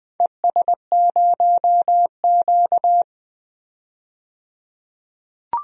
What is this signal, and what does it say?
Morse code, heard as a keyed tone, about 20 wpm, 700 Hz. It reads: ES0Q
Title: DXCC-100 Call Signs - 4 Characters - 20wpm